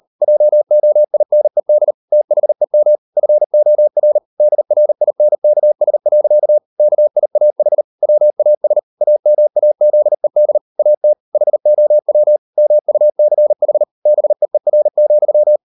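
Morse code, heard as a keyed tone, about 40 wpm, 600 Hz. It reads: JOINED THEM FOR DRINKS. KIAH WAS AMAZED AT HOW MUCH BEER,
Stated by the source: Morse code practice at 38 WPM, sending Ken Follett's novel Never in English, from CW med Gnister.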